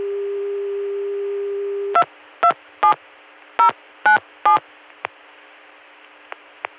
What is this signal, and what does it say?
DTMF, heard as telephone touch tones, tones of about 110 ms. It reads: 227097